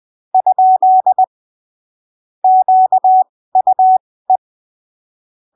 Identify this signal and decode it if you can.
Morse code, heard as a keyed tone, about 20 wpm, spaced as for 15 wpm, 750 Hz. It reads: ? QUE
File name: arquivos / texto / texto 16 20ppm